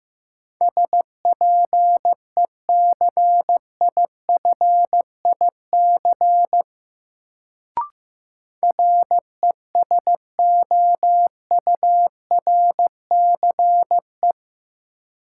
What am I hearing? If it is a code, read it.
Morse code, heard as a keyed tone, about 15 wpm, 700 Hz. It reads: SPECIFIC RESOURCE
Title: Sentences from Top 1300 Words - New Word Drill Alt - 15wpm